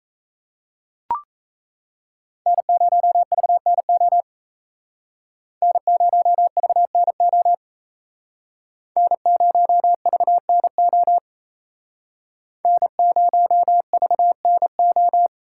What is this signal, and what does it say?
Morse code, heard as a keyed tone, about 35 wpm, 700 Hz. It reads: N0VNO N0VNO N0VNO N0VNO
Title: Speed Racing - USA 1x3 Call Signs 28wpm